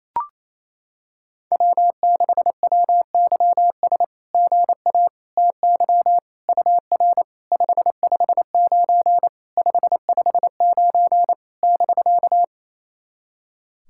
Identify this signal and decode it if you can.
Morse code, heard as a keyed tone, about 29 wpm, 700 Hz. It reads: W6WYS GA TY UR 559 559 <BK>
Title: SOTA QSO Practice 28wpm